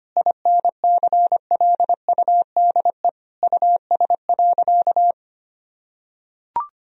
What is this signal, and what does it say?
Morse code, heard as a keyed tone, about 25 wpm, 700 Hz. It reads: INCLUDE US.
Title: Sentences from Top 1000 Words - Review Part 3 - 25wpm